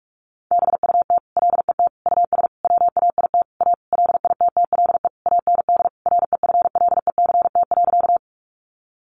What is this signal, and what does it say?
Morse code, heard as a keyed tone, about 44 wpm, 700 Hz. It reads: BUT LET US WAIT A LITTLE AND REFLECT.
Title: Sentences from Top 1500 Words 45wpm